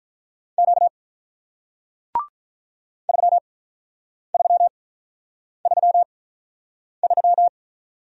Morse code, heard as a keyed tone, about 47 wpm, spaced as for 9 wpm, 700 Hz. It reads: Q 3333